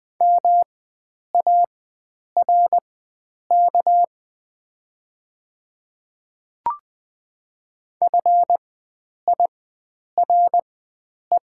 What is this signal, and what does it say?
Morse code, heard as a keyed tone, about 20 wpm, spaced as for 9 wpm, 690 Hz. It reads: MARK FIRE
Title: 4x Character Spacing - Top 500 Words - 20wpm